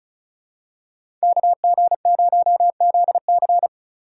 Morse code, heard as a keyed tone, about 34 wpm, 700 Hz. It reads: KG0ZC
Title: USA Call Signs - 5 Characters - 2x Character Spacing - 35wpm